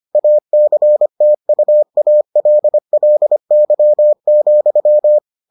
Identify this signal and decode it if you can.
Morse code, heard as a keyed tone, about 25 wpm, 600 Hz. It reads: ACTUALLY,